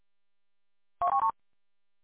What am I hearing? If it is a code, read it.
There is no code.